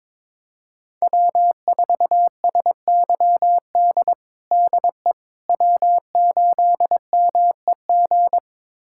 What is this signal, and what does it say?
Morse code, heard as a keyed tone, about 22 wpm, 700 Hz. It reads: W4SYD DE W8MEG
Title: USA Callsign DE Callsign - 5 Characters - 22wpm